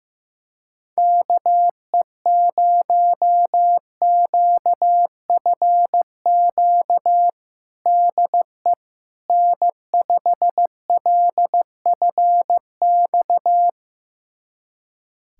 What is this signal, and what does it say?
Morse code, heard as a keyed tone, about 15 wpm, 700 Hz. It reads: KE0QFQ DE N5LFX
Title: USA Callsign DE Callsign 15wpm